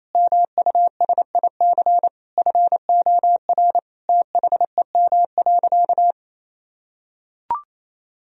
Morse code, heard as a keyed tone, about 28 wpm, 700 Hz. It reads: MUSIC FOR THEM.
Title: Sentences from Top 1000 Words - Review Part 1 - 28wpm